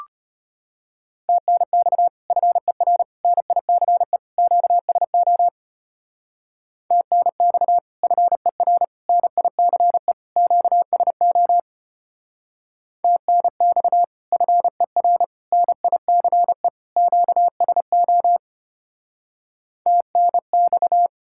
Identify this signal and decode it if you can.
Morse code, heard as a keyed tone, about 30 wpm, 700 Hz. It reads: TNX FER NICE QSO TNX FER NICE QSO TNX FER NICE QSO TNX
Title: Speed-Racing Common QSO Phrases - 25wpm